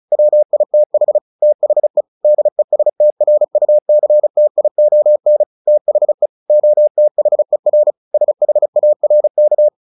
Morse code, heard as a keyed tone, about 35 wpm, 600 Hz. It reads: WITH THE DESTRUCTION THE OTHER SHARK
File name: the old man and the sea 2 35wpm